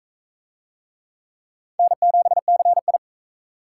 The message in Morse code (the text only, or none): N7CI